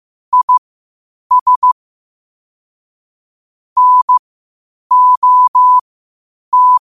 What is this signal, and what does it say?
Morse code, heard as a keyed tone, about 15 wpm, spaced as for 8 wpm, 1000 Hz. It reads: IS NOT
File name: qotdfiles / 8wpm